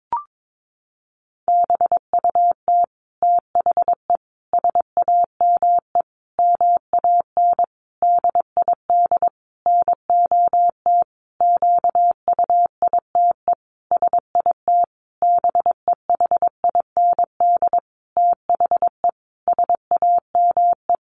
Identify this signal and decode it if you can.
Morse code, heard as a keyed tone, about 22 wpm, 700 Hz. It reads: BUT THE SAME MAN DID NOT QUITE SIT BEHIND THE SAME
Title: A Sound Of Thunder 22wpm